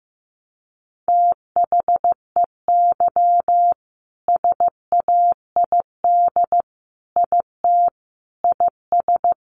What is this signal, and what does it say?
Morse code, heard as a keyed tone, about 15 wpm, 700 Hz. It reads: THEY SAID IT IS
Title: Sentences from Top 1100 Words 15wpm